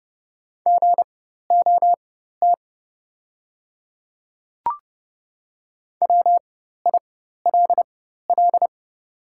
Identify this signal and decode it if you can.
Morse code, heard as a keyed tone, about 30 wpm, spaced as for 14 wpm, 700 Hz. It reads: GOT WILL
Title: 4x Character Spacing - Top 300 Words - 30wpm